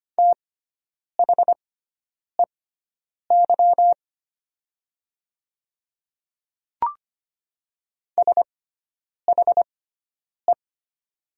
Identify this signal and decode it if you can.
Morse code, heard as a keyed tone, about 25 wpm, spaced as for 9 wpm, 700 Hz. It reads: THEY SHE
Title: 6x Character Spacing - Top 500 Words - 25wpm